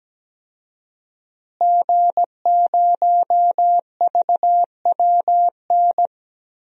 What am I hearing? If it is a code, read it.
Morse code, heard as a keyed tone, about 17 wpm, 700 Hz. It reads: G0VWN
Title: DXCC-100 Call Signs - 5 Characters - Repeated 2x - 17wpm